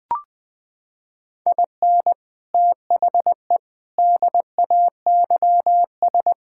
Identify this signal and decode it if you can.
Morse code, heard as a keyed tone, about 21 wpm, 700 Hz. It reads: IN THE DAYS
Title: Sets of 6 Words - 20wpm